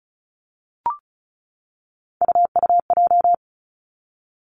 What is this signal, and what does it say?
Morse code, heard as a keyed tone, about 36 wpm, 700 Hz. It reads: UUJ
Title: Three Challenging Letter-Number Combinations 35wpm